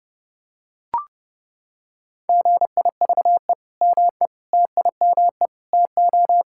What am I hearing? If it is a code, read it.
Morse code, heard as a keyed tone, about 30 wpm, 700 Hz. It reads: GIVE ME TIME TO
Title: Sentences from Top 1600 Words 30wpm